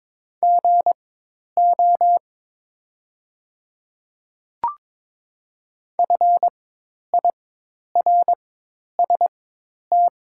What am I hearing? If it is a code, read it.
Morse code, heard as a keyed tone, about 22 wpm, spaced as for 10 wpm, 700 Hz. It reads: GO FIRST
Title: 4x Character Spacing - Top 100 Words - 22wpm